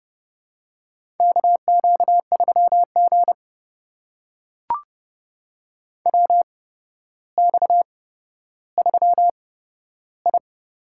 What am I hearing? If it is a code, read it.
Morse code, heard as a keyed tone, about 30 wpm, spaced as for 13 wpm, 700 Hz. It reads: KQ3G WX3I
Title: USA Call Signs - 4 Characters - 8x Character Spacing - 30wpm